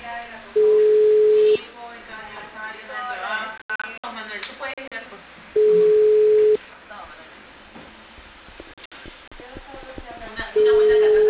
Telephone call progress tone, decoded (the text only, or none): ringback tone